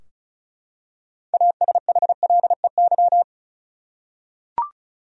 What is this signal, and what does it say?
Morse code, heard as a keyed tone, about 34 wpm, 700 Hz. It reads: ASHLEY
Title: USA Common Names - 35wpm